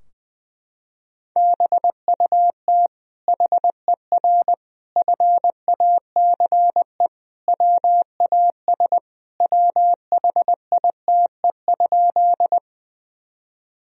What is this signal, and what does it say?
Morse code, heard as a keyed tone, about 20 wpm, 700 Hz. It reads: BUT HER FACE WAS WHITE?
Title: Sentences from Top 1000 Words - Review Part 2 - 20wpm